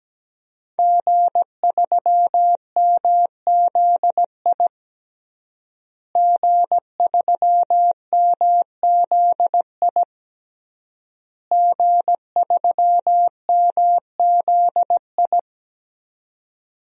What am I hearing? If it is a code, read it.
Morse code, heard as a keyed tone, about 17 wpm, 700 Hz. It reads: G3MZI G3MZI G3MZI
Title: DXCC-100 Call Signs - 5 Characters - Repeated 3x - 17wpm